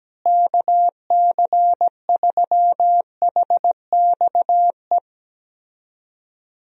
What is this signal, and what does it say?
Morse code, heard as a keyed tone, about 17 wpm, 700 Hz. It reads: KC3HXE
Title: USA Call Signs - 6 Characters - Repeated 3x - 17wpm